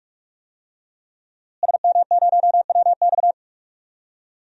Morse code, heard as a keyed tone, about 41 wpm, 700 Hz. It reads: SM0WX
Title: DXCC-100 Call Signs - 5 Characters - Repeated 2x - 45wpm